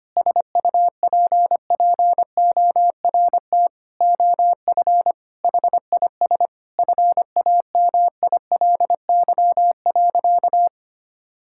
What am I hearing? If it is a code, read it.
Morse code, heard as a keyed tone, about 25 wpm, 700 Hz. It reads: SUPPORT OF HIS FAMILY.
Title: Sentences from Top 1000 Words - Review Part 5 - 25wpm